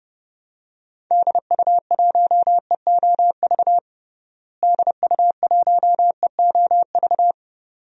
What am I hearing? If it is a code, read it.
Morse code, heard as a keyed tone, about 30 wpm, 700 Hz. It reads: DU1EOV DU1EOV
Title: DXCC-100 Call Signs - 6 Characters - Repeated 2x - 30wpm